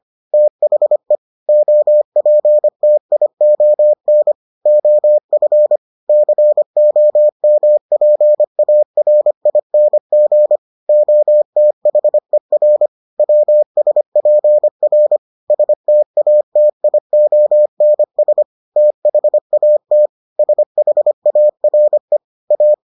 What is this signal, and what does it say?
Morse code, heard as a keyed tone, about 25 wpm, 600 Hz. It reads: THE OPTION OF COMPARING OTHER WSPR STATIONS THAT SHARE A